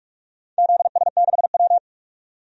Morse code, heard as a keyed tone, about 44 wpm, 700 Hz. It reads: ZS6W
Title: DXCC-100 Call Signs - 4 Characters - 45wpm